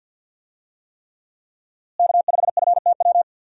Morse code, heard as a keyed tone, about 46 wpm, 700 Hz. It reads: K5FTW
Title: USA Call Signs - 5 Characters - Repeated 3x - 50wpm